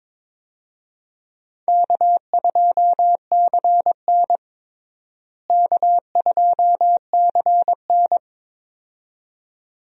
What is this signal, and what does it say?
Morse code, heard as a keyed tone, about 22 wpm, 700 Hz. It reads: K2CN K2CN
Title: USA Call Signs - 4 Characters - Repeated 2x - 22wpm